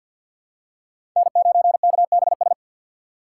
Morse code, heard as a keyed tone, about 52 wpm, 700 Hz. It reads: N9KBS
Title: USA Call Signs - 5 Characters - 2x Character Spacing - 50wpm